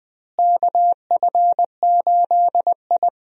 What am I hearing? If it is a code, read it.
Morse code, heard as a keyed tone, about 20 wpm, 700 Hz. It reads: KF8I
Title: USA Call Signs - 4 Characters - Repeated 2x - 20wpm